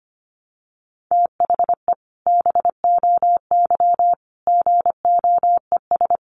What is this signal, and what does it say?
Morse code, heard as a keyed tone, about 25 wpm, 700 Hz. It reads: THE BOY GOES
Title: Additional Easy Phrases 25wpm